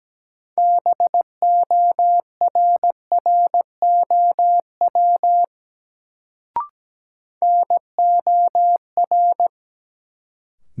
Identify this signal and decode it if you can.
Morse code, heard as a keyed tone, about 17 wpm, 700 Hz. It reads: BORROW NOR
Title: Sentences from Top 1500 Words - New Word Drill Alt - 17wpm